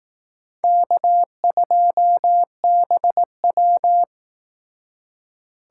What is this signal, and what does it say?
Morse code, heard as a keyed tone, about 18 wpm, 700 Hz. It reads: K2BW
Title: Speed Racing - USA Call Signs 15wpm